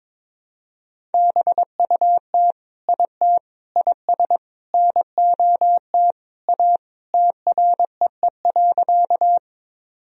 Morse code, heard as a keyed tone, about 22 wpm, 700 Hz. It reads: BUT IT IS NOT A TREE.
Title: Sentences from Top 1000 Words - Review Part 1 - 22wpm